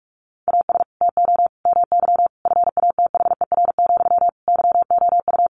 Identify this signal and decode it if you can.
Morse code, heard as a keyed tone, about 44 wpm, 700 Hz. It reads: AS TO MY FATHER, YOU
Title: Sentences from Top 600 Words 45wpm